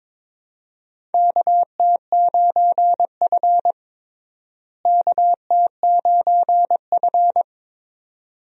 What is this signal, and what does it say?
Morse code, heard as a keyed tone, about 21 wpm, 700 Hz. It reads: KT9F KT9F
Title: USA Call Signs - 4 Characters - Repeated 2x - 22wpm